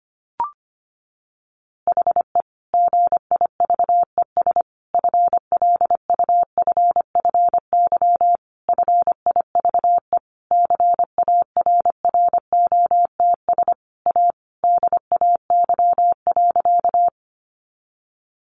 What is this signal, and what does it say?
Morse code, heard as a keyed tone, about 25 wpm, 700 Hz. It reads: HE GIVES FLUFFY FIVE CARROTS A DAY.
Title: Easy Phrases 25wpm